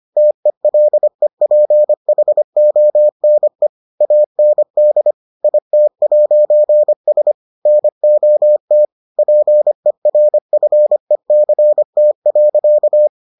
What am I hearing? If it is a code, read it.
Morse code, heard as a keyed tone, about 25 wpm, 600 Hz. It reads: TELEPHONE AND IT'S NOT PERFECT.